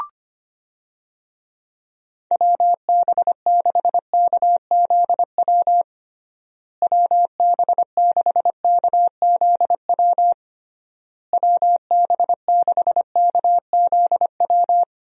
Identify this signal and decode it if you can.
Morse code, heard as a keyed tone, about 25 wpm, 700 Hz. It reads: WB6KZW WB6KZW WB6KZW
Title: USA Call Signs - 6 Characters - Repeated 3x - 25wpm